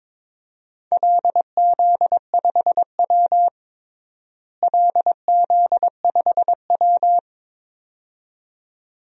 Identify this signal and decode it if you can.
Morse code, heard as a keyed tone, about 22 wpm, 700 Hz. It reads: LZ5W LZ5W
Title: DXCC-100 Call Signs - 4 Characters - Repeated 2x - 22wpm